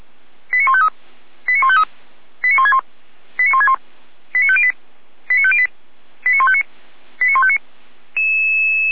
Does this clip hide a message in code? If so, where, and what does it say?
Yes: five-tone selective call (ZVEI) from 0.5 to 7.6 s, heard as quick runs of five tones, tones of about 70 ms. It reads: 89153 89153 89161 89161 89508 89508 89159 89159